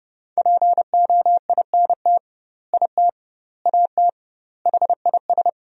Morse code, heard as a keyed tone, about 32 wpm, 700 Hz. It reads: POINT IT AT HIS
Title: 2x Word-Spacing - Sentences from Top 300 Words - 30wpm